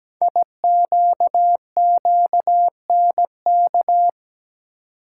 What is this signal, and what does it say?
Morse code, heard as a keyed tone, about 17 wpm, 700 Hz. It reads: IQQNK
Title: Code Groups - 5 Letters - 17wpm